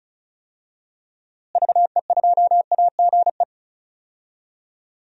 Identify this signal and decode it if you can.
Morse code, heard as a keyed tone, about 36 wpm, 700 Hz. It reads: VE2AGE
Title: DXCC-100 Call Signs - 6 Characters - Repeated 3x - 35wpm